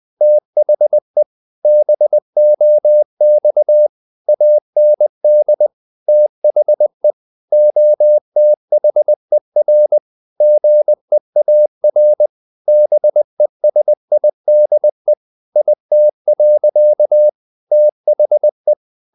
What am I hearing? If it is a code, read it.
Morse code, heard as a keyed tone, about 20 wpm, 600 Hz. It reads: THE BOX AND THE OTHER GEAR BESIDE IT. THE
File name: the old man and the sea / the old man and the sea part 1 20wpm